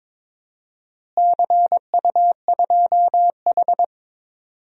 Morse code, heard as a keyed tone, about 21 wpm, 700 Hz. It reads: CU2H
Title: DXCC-100 Call Signs - 4 Characters - Repeated 3x - 22wpm